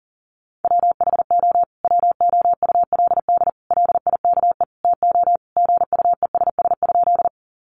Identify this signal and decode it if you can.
Morse code, heard as a keyed tone, about 40 wpm, 700 Hz. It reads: WHO WOULD LIKE TO GUESS?
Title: Sentences from Top 1500 Words 40wpm